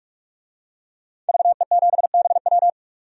Morse code, heard as a keyed tone, about 46 wpm, 700 Hz. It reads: VE7BW
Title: DXCC-100 Call Signs - 5 Characters - 45wpm